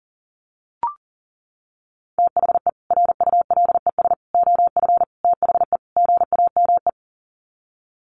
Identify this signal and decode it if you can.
Morse code, heard as a keyed tone, about 37 wpm, 700 Hz. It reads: THE RULES OF THE GAME
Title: Sets of 5 Words Encore2 40wpm